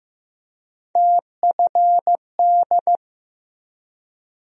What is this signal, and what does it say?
Morse code, heard as a keyed tone, about 15 wpm, 700 Hz. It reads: TFD
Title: USA Call Sign Postfixes - 3 Letters 15wpm